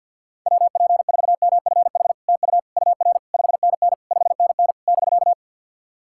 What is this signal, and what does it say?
Morse code, heard as a keyed tone, about 48 wpm, 690 Hz. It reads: WP4MFH TU UR 5NN 5NN <BK>
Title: SOTA QSO Practice 50wpm